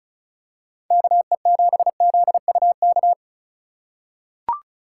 Morse code, heard as a keyed tone, about 36 wpm, 700 Hz. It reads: KE7ZUK
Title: USA Call Signs - 3x Character Spacing - 35wpm